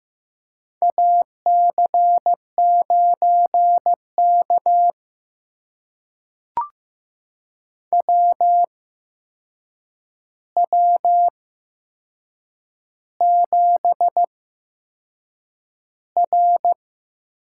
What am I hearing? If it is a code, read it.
Morse code, heard as a keyed tone, about 15 wpm, spaced as for 6 wpm, 700 Hz. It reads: AC9K WW7R
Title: USA Call Signs - 8x Character Spacing - 15wpm